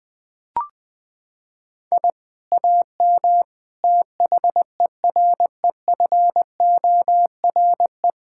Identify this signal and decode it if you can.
Morse code, heard as a keyed tone, about 20 wpm, 700 Hz. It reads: I AM THEREFORE
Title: Sentences from Top 1000 Words - Review Part 6 - 20wpm